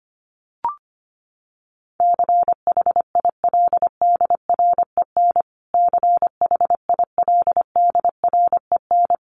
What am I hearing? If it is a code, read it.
Morse code, heard as a keyed tone, about 25 wpm, 700 Hz. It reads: CHILDREN CHILDREN
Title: Top 300 Words Repeated 25wpm